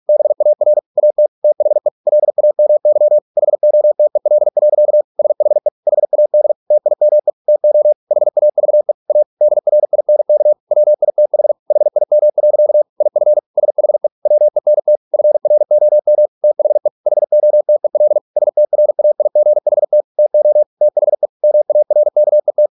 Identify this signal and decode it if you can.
Morse code, heard as a keyed tone, about 46 wpm, 600 Hz. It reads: BAR AT THE LAMY HOTEL. SHE HAD TIME TO HAVE A DRINK WITH HIM. IF SHE WENT FROM THE HOTEL STRAIGHT TO THE MARKET